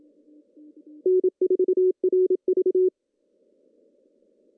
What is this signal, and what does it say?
Morse code, heard as a keyed tone, about 26 wpm, 380 Hz. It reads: N4RV
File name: CW / cqww-cw-k5tr